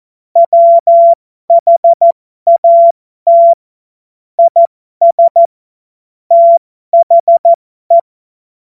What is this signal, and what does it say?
Morse code, heard as a keyed tone, about 14 wpm, spaced as for 12 wpm, 680 Hz. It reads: WHAT IS THE
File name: Morsekurs / Witze